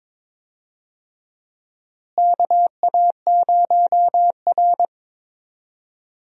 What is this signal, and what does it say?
Morse code, heard as a keyed tone, about 22 wpm, 700 Hz. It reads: KA0R
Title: USA Call Signs - 4 Characters - Repeated 3x - 22wpm